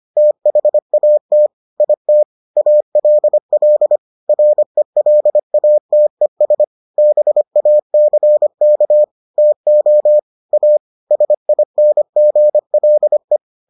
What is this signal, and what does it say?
Morse code, heard as a keyed tone, about 25 wpm, 600 Hz. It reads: THAT IT ALL RELATES BACK TO A SINGLE